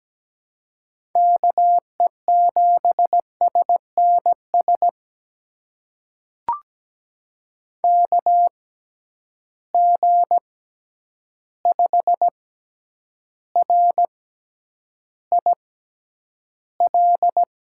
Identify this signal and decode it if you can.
Morse code, heard as a keyed tone, about 17 wpm, spaced as for 9 wpm, 700 Hz. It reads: KE7SNS KG5RIL
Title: USA Call Signs - 6 Characters - 6x Character Spacing - 17wpm